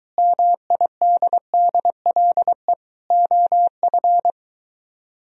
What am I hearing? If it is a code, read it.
Morse code, heard as a keyed tone, about 23 wpm, 700 Hz. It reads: MIDDLE OF